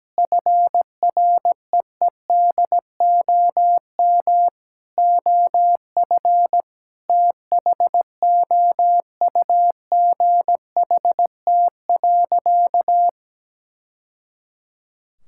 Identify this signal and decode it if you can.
Morse code, heard as a keyed tone, about 17 wpm, 700 Hz. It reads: FREEDOM OF THOUGHT.
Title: Sentences from Top 1000 Words - Review Part 5 - 17wpm